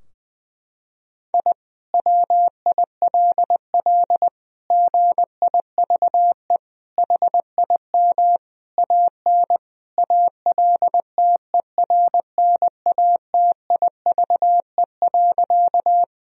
Morse code, heard as a keyed tone, about 20 wpm, 700 Hz. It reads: I WILL GIVE HIM AN ALTERNATIVE.
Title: Sentences from Top 1000 Words - Review Part 4 - 20wpm